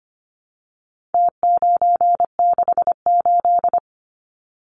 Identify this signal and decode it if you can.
Morse code, heard as a keyed tone, about 25 wpm, 700 Hz. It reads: T968
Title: Numbers - Contest Style - 4-Digit 25wpm